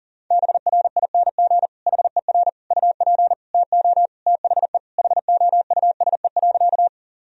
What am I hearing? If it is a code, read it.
Morse code, heard as a keyed tone, about 40 wpm, 700 Hz. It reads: BRING HER UP TO THE HOUSE.
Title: Sentences from Top 1000 Words - Review Part 3 - 40wpm